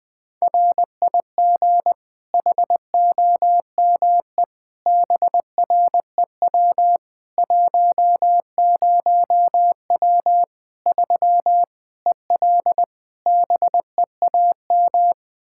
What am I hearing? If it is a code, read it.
Morse code, heard as a keyed tone, about 21 wpm, 700 Hz. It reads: RIG HOME BREW 10W 3 EL BEAM
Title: K1USN SST QSO Practice 20wpm